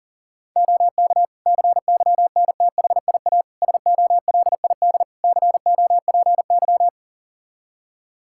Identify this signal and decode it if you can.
Morse code, heard as a keyed tone, about 40 wpm, 700 Hz. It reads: OK CYNTHIA SOLID COPY
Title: Practice QSO - Exchange 3 - More Info - 40wpm